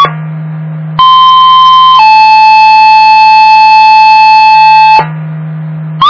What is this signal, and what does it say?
Two-tone page, heard as a one-second tone, then a three-second tone, about 1040 then 870 Hz. It is heard from 1.0 to 5.0 s.